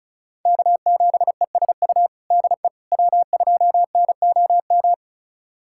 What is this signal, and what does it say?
Morse code, heard as a keyed tone, about 34 wpm, 700 Hz. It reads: K7ESU DE W2NOM
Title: USA Callsign DE Callsign - 5 Characters - 35wpm